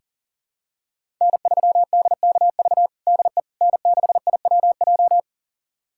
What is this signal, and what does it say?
Morse code, heard as a keyed tone, about 40 wpm, 700 Hz. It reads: N3DKV DE N6IWJ
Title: USA Callsign DE Callsign - 5 Characters - 40wpm